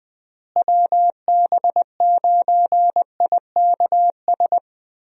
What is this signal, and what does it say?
Morse code, heard as a keyed tone, about 20 wpm, 700 Hz. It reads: WB9IKS
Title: USA Call Signs - 6 Characters - Repeated 2x - 20wpm